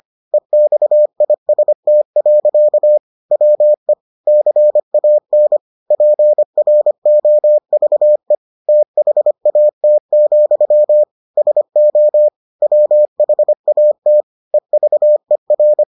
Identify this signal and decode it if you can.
Morse code, heard as a keyed tone, about 25 wpm, 600 Hz. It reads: EXIST. WE CAN PROVE THAT, SO WHAT EVER